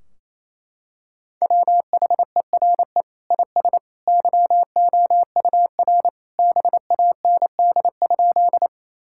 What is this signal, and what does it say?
Morse code, heard as a keyed tone, about 28 wpm, 700 Hz. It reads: WHERE IS YOUR BAND?
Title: Sentences from Top 1600 Words 28wpm